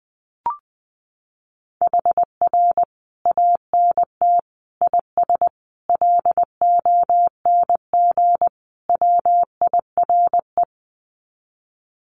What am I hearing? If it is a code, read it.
Morse code, heard as a keyed tone, about 20 wpm, 700 Hz. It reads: HR ANT IS LONG WIRE
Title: Common QSO Phrases 20wpm